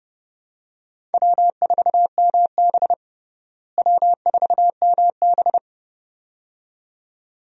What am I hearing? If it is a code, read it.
Morse code, heard as a keyed tone, about 30 wpm, 700 Hz. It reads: W4MB W4MB
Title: USA Call Signs - 4 Characters - Repeated 2x - 30wpm